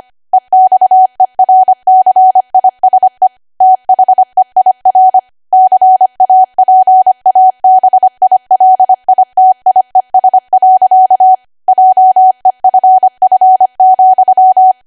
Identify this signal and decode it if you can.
Morse code, heard as a keyed tone, about 25 wpm, 750 Hz. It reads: EXERCISE THEIR CAPABILITIES. JEFF,